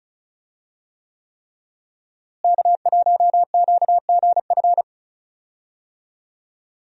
Morse code, heard as a keyed tone, about 36 wpm, 700 Hz. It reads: K1QGF
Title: USA Call Signs - 5 Characters - Repeated 3x - 35wpm